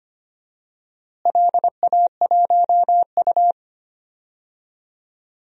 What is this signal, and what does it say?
Morse code, heard as a keyed tone, about 25 wpm, 700 Hz. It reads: LA1U
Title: DXCC-100 Call Signs - 4 Characters - Repeated 3x - 25wpm